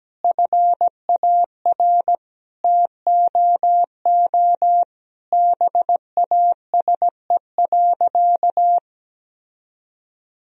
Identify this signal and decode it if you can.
Morse code, heard as a keyed tone, about 17 wpm, 700 Hz. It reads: FAR TOO BASE.
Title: Sentences from Top 1000 Words - Review Part 1 - 17wpm